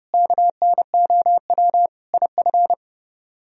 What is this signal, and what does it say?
Morse code, heard as a keyed tone, about 40 wpm, 700 Hz. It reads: KNOW IF